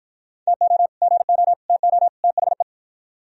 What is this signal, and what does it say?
Morse code, heard as a keyed tone, about 54 wpm, 700 Hz. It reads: TO GO TO THE